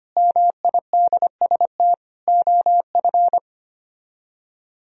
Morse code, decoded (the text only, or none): MIDST OF